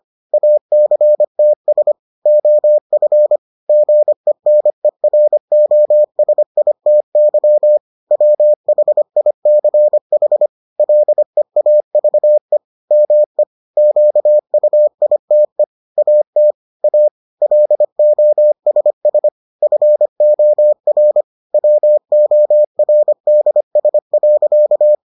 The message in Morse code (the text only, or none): ACTS OF GENEROSITY WHICH LEAVE ME QUITE AT A LOSS FOR WORDS.